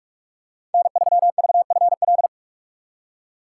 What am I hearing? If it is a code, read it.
Morse code, heard as a keyed tone, about 46 wpm, 700 Hz. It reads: N3VFL
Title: USA Call Signs - 2x Character Spacing - 45wpm